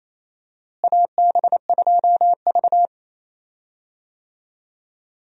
Morse code, heard as a keyed tone, about 27 wpm, 700 Hz. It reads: AB2V
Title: USA Call Signs - 4 Characters - Repeated 2x - 28wpm